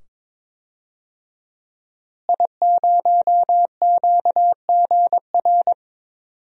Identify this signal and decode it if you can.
Morse code, heard as a keyed tone, about 22 wpm, 700 Hz. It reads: I0QGR
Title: DXCC-100 Call Signs - 5 Characters - Repeated 2x - 22wpm